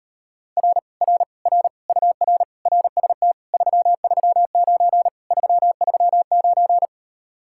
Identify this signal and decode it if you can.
Morse code, heard as a keyed tone, about 38 wpm, 700 Hz. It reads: R R R UR RST 339 339